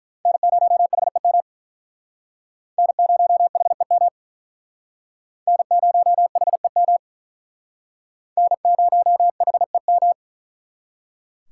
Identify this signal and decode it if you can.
Morse code, heard as a keyed tone, about 43 wpm, 700 Hz. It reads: N0HEM N0HEM N0HEM N0HEM
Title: Speed Racing - USA 1x3 Call Signs 35wpm